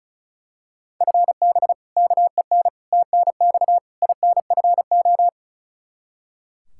Morse code, heard as a keyed tone, about 36 wpm, 700 Hz. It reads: FB KEN TNX INFO
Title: Practice QSO - Exchange 3 - More Info - 35wpm